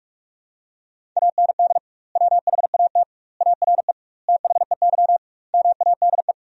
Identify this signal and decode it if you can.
Morse code, heard as a keyed tone, about 42 wpm, 700 Hz. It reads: AND WHAT ARE THEY MADE
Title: 2x Word-Spacing - Sentences from Top 200 Words - 45wpm